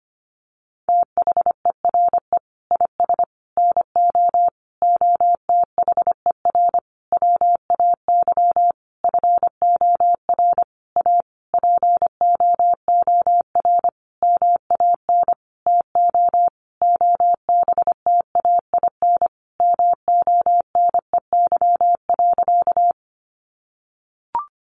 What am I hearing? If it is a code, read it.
Morse code, heard as a keyed tone, about 25 wpm, 700 Hz. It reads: THERE IS NO OTHER WAY FOR A POOR MAN TO OBTAIN MONEY.
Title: Sentences from Top 1300 Words 25wpm